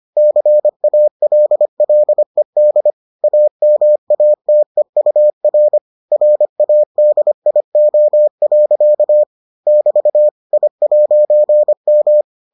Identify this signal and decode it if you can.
Morse code, heard as a keyed tone, about 25 wpm, 600 Hz. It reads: CALLED AMATEUR RADIO. <BT> I'M